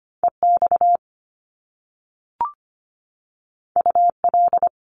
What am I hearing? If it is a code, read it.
Morse code, heard as a keyed tone, about 25 wpm, 700 Hz. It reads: EX UL
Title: Top Two Letter Combinations in Words 25wpm